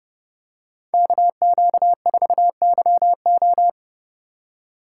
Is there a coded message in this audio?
Yes: Morse code from 0.9 to 3.7 s, heard as a keyed tone, about 30 wpm, 700 Hz. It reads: KQ4YO